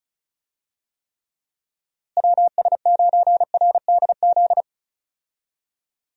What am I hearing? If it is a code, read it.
Morse code, heard as a keyed tone, about 36 wpm, 700 Hz. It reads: WS9RDZ
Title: USA Call Signs - 6 Characters - Repeated 2x - 35wpm